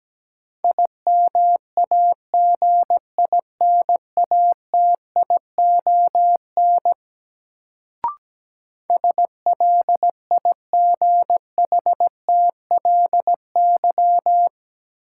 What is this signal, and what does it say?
Morse code, heard as a keyed tone, about 17 wpm, 700 Hz. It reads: IMAGINATION SLIGHTLY
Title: Sentences from Top 1300 Words - New Word Drill Alt - 17wpm